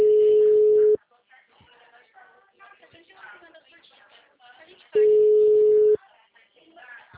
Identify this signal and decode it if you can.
Ringback tone (Europe, 425 Hz), with speech.